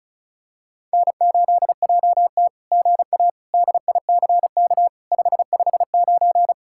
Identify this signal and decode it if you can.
Morse code, heard as a keyed tone, about 33 wpm, 700 Hz. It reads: N8JT GA DICK 559
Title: SOTA QSO Practice 35wpm